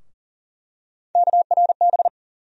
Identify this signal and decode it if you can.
Morse code, heard as a keyed tone, about 38 wpm, 700 Hz. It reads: KRB
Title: USA Call Sign Postfixes - 3 Letters 40wpm